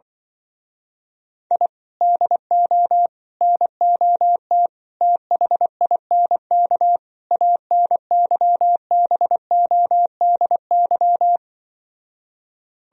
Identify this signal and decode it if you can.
Morse code, heard as a keyed tone, about 24 wpm, 700 Hz. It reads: I DO NOT THINK ANYBODY
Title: Speed-Racing - Sets of 5 Words 20wpm